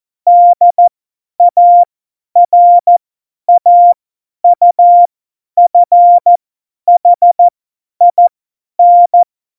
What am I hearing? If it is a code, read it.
Morse code, heard as a keyed tone, about 14 wpm, spaced as for 10 wpm, 700 Hz. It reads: DARAUFHIN
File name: Morsekurs / Witze / Witz 27. November 2018, 50 F70